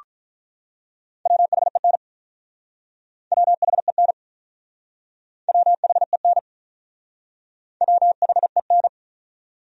Morse code, heard as a keyed tone, about 47 wpm, 700 Hz. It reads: WHEN WHEN WHEN WHEN